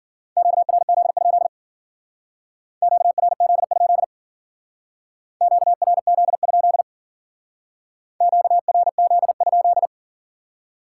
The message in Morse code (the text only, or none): QRZ? QRZ? QRZ? QRZ?